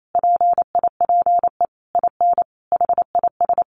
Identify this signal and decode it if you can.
Morse code, heard as a keyed tone, about 28 wpm, 700 Hz. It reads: PIPE IN HIS
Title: Sentences from Top 1500 Words 28wpm